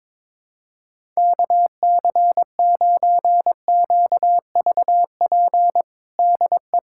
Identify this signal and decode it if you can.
Morse code, heard as a keyed tone, about 21 wpm, 700 Hz. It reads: KC9QVP DE
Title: USA Callsign DE Callsign - 6 Characters - 22wpm